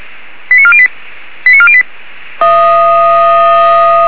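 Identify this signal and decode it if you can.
Five-tone selective call (ZVEI), heard as quick runs of five tones, tones of about 70 ms. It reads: 89408 89408
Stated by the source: ZVEI five-tone selective call 89406, received by SDR.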